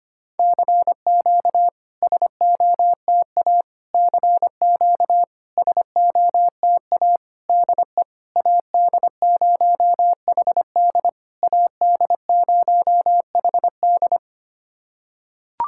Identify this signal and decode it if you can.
Morse code, heard as a keyed tone, about 25 wpm, 690 Hz. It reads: CQ SOTA CQ SOTA DE AD0HD AD0HD
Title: SOTA QSO Practice 25wpm